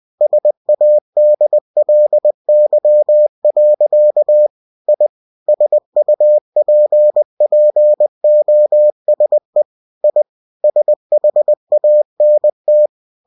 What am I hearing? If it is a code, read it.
Morse code, heard as a keyed tone, about 20 wpm, 600 Hz. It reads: SADLY. I SUPPOSE I SHANT
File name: cw-books / winnie-the-pooh / winnie-20wpm-page-0063